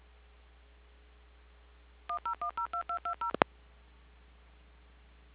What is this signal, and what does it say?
DTMF, heard as telephone touch tones, tones of about 110 ms. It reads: 10102220